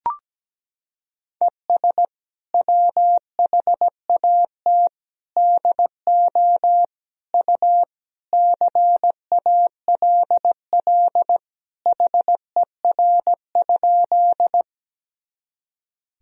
Morse code, heard as a keyed tone, about 17 wpm, 700 Hz. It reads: ES WHAT DO U CALL HER?
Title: Sentences from Top 100 Words and Abbreviations 17wpm